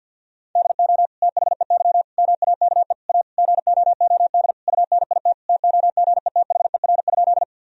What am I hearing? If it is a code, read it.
Morse code, heard as a keyed tone, about 50 wpm, 700 Hz. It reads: DO THEY MAKE A GOOD UNIT TOGETHER?
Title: Sentences from Top 1000 Words - Review Part 2 - 50wpm